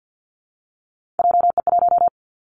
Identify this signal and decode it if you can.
Morse code, heard as a keyed tone, about 49 wpm, 700 Hz. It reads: JE1
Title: Callsign Prefixes for Top 25 DXCC - Repeated 3x - 50wpm